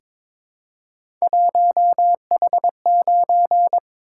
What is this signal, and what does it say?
Morse code, heard as a keyed tone, about 22 wpm, 700 Hz. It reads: 1H9
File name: practice / 22wpm / Three Letter-Number Combinations - 22wpm